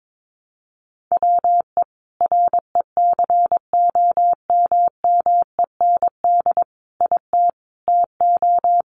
Morse code, heard as a keyed tone, about 22 wpm, 700 Hz. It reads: WE RECOMMEND IT TO
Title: Sentences from Top 1200 Words 22wpm